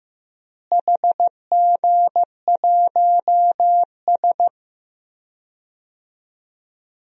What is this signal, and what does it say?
Morse code, heard as a keyed tone, about 15 wpm, 700 Hz. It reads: HG1S
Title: DXCC-100 Call Signs - 4 Characters - Repeated 2x - 15wpm